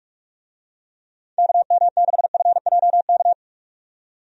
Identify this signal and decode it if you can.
Morse code, heard as a keyed tone, about 46 wpm, 700 Hz. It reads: KM6FJK
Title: USA Call Signs - 6 Characters - 45wpm